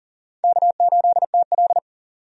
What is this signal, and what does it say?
Morse code, heard as a keyed tone, about 40 wpm, 700 Hz. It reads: K8TL
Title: USA Call Signs - 4 Characters - 40wpm